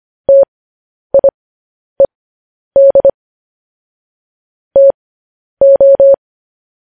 Morse code, heard as a keyed tone, about 25 wpm, spaced as for 10 wpm, 550 Hz. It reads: TIED TO